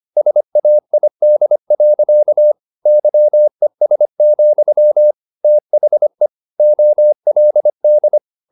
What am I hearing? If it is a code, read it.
Morse code, heard as a keyed tone, about 25 wpm, 600 Hz. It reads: SAID. YES, THE OLD